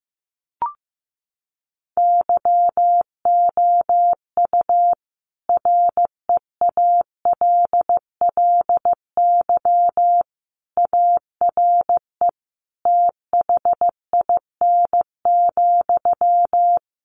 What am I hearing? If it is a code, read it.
Morse code, heard as a keyed tone, about 15 wpm, 700 Hz. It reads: YOU REALLY ARE THIN,
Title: Sentences from Top 1200 Words 15wpm